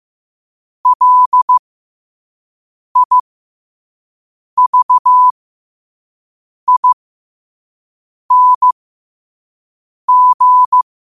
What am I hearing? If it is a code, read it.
Morse code, heard as a keyed tone, about 15 wpm, spaced as for 5 wpm, 1000 Hz. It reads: LIVING